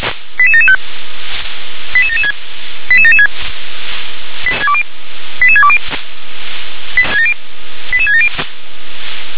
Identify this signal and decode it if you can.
Five-tone selective call (ZVEI), heard as quick runs of five tones, tones of about 70 ms. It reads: 88775 88776 88776 88620 88670
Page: SDR/ZVEI